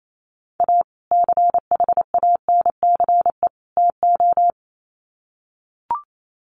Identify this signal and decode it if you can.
Morse code, heard as a keyed tone, about 28 wpm, 700 Hz. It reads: A CHANCE TO
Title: Sets of 3 Words 28wpm